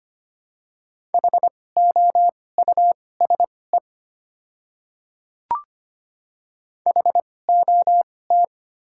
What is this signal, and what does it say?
Morse code, heard as a keyed tone, about 25 wpm, spaced as for 18 wpm, 700 Hz. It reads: HOUSE HOT
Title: 2x Character Spacing - Top 200 Words - 25wpm